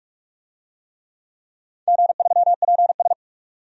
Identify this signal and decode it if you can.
Morse code, heard as a keyed tone, about 43 wpm, 700 Hz. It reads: G3PS
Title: DXCC-100 Call Signs - 4 Characters - 45wpm